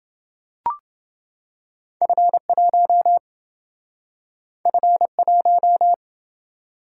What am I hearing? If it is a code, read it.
Morse code, heard as a keyed tone, about 28 wpm, 700 Hz. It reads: F1 F1